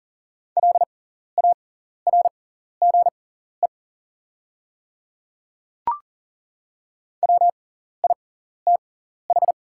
Morse code, heard as a keyed tone, about 40 wpm, spaced as for 14 wpm, 690 Hz. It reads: LARGE WITH
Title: 6x Character Spacing - Top 300 Words - 40wpm